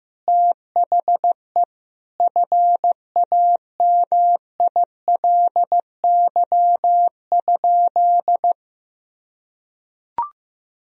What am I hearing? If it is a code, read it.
Morse code, heard as a keyed tone, about 15 wpm, 700 Hz. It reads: THE FAMILY?
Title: Sentences from Top 1000 Words - Review Part 4 - 15wpm